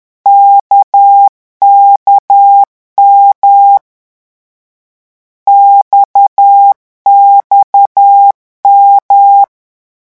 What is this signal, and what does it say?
Morse code, heard as a keyed tone, about 10 wpm, 800 Hz. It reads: KKM XXM
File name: morse mp3 / lecon1-vitesse 10